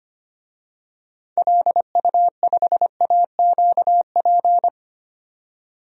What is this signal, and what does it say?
Morse code, heard as a keyed tone, about 25 wpm, 700 Hz. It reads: LU5AQP
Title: DXCC-100 Call Signs - 6 Characters - 25wpm